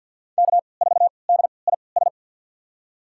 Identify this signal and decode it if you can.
Morse code, heard as a keyed tone, about 52 wpm, spaced as for 29 wpm, 700 Hz. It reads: K4DIS
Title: USA Call Signs - 5 Characters - 3x Character Spacing - 50wpm